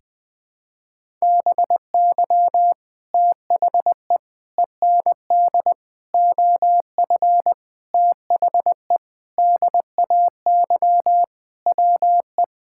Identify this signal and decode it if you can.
Morse code, heard as a keyed tone, about 21 wpm, 700 Hz. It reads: BY THE END OF THE DAY WE
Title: Sentences with 4-Word Phrases - 20wpm